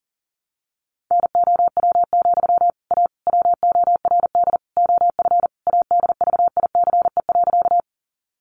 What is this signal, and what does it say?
Morse code, heard as a keyed tone, about 40 wpm, 700 Hz. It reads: NOW, A WORD OF ADVICE.
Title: Sentences from Top 1100 Words 40wpm